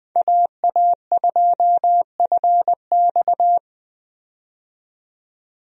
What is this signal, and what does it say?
Morse code, heard as a keyed tone, about 20 wpm, 700 Hz. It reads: AA2FX
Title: USA Call Signs - 5 Characters - 20wpm